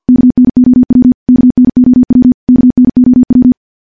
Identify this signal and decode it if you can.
Morse code, heard as a keyed tone, about 50 wpm, 250 Hz. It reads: KNOW KNOW KNOW
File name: morse / morse-training-50WPM